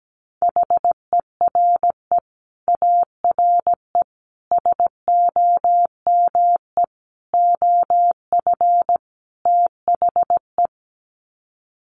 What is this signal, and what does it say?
Morse code, heard as a keyed tone, about 17 wpm, 700 Hz. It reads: HERE ARE SOME OF THE
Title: Sets of 5 Words Encore 17wpm